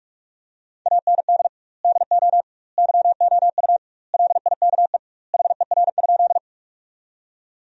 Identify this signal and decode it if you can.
Morse code, heard as a keyed tone, about 41 wpm, 700 Hz. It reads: AND DO YOU LIKE HER?
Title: 2x Word-Spacing - Sentences from Top 100 Words - 45wpm